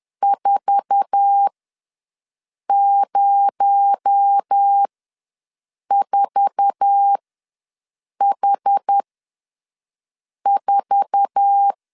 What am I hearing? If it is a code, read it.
Morse code, heard as a keyed tone, about 10 wpm, spaced as for 5 wpm, 800 Hz. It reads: 404H4
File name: LEZIONI / 8 YJ45 / 8 YJ4 ws 10wpm 4020